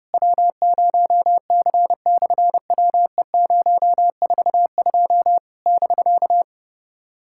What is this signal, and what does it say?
Morse code, heard as a keyed tone, about 30 wpm, 700 Hz. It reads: W0C/WE042 <BK>
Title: SOTA QSO Practice 30wpm